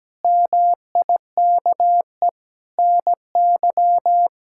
Morse code, heard as a keyed tone, about 17 wpm, 700 Hz. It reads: MIKE NY
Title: K1USN SST QSO Practice - 17wpm